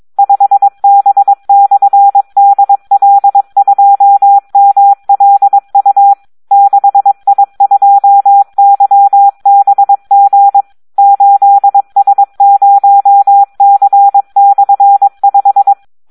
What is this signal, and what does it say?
Morse code, heard as a keyed tone, about 22 wpm, 800 Hz. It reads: 5B/DL2MLU 6I2YBG 8S0C/5
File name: Audio practica CW / SD 1 22